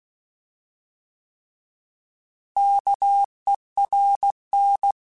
Morse code, heard as a keyed tone, about 16 wpm, 800 Hz. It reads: KERN
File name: Alphabetical / Alphabetical mp3s / Ks @15